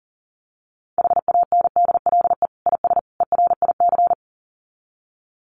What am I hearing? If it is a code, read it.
Morse code, heard as a keyed tone, about 40 wpm, 700 Hz. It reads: HANDLE IS ERIC
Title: Common QSO Phrases 40wpm